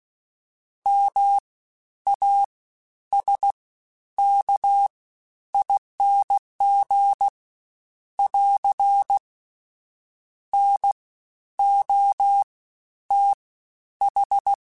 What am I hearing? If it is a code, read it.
Morse code, heard as a keyed tone, about 16 wpm, spaced as for 10 wpm, 800 Hz. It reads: MASKING <AR> NOTH
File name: Special INGs-mp3s / Special INGs-mp3s / Fast ing @15